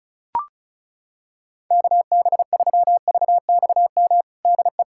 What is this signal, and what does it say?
Morse code, heard as a keyed tone, about 34 wpm, 700 Hz. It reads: KB3VXM DE
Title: USA Callsign DE Callsign 35wpm